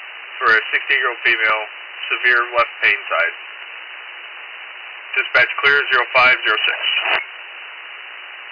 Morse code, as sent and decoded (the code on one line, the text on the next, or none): none